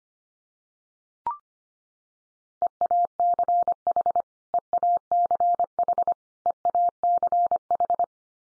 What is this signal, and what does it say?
Morse code, heard as a keyed tone, about 25 wpm, 700 Hz. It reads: EACH EACH EACH
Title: Top 100 Words 25wpm x3